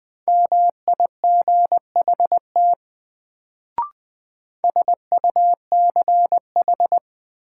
Morse code, heard as a keyed tone, about 20 wpm, 700 Hz. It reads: MIGHT SUCH
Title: Sets of 2 Words - New Word Drill - 20wpm